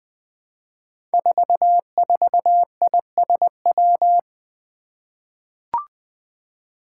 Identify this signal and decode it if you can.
Morse code, heard as a keyed tone, about 20 wpm, 700 Hz. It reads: 44ISW
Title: Five Challenging Letter-Number Combinations 20wpm